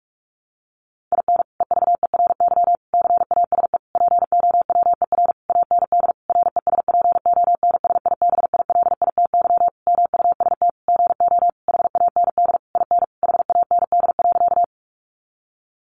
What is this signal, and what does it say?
Morse code, heard as a keyed tone, about 44 wpm, 700 Hz. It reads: IN EVERY CASE POWER AND RESPONSIBILITY MUST GO HAND IN HAND.
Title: Sentences from Top 1200 Words 45wpm